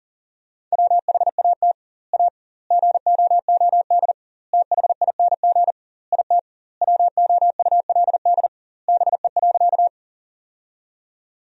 Morse code, heard as a keyed tone, about 37 wpm, 700 Hz. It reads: WHAT A GOOD THING IT WOULD BE.
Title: 2x Word-Spacing - Sentences from Top 200 Words - 40wpm